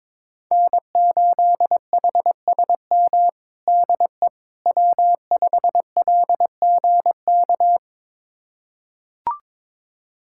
Morse code, heard as a keyed tone, about 22 wpm, 700 Hz. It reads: N8HSM DE W5LGK
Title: USA Callsign DE Callsign - 5 Characters - 22wpm